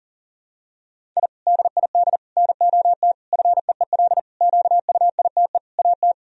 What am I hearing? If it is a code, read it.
Morse code, heard as a keyed tone, about 40 wpm, 700 Hz. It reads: I DID NOT FEEL QUITE AT
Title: Sentences from Top 1000 Words - Review Part 1 - 40wpm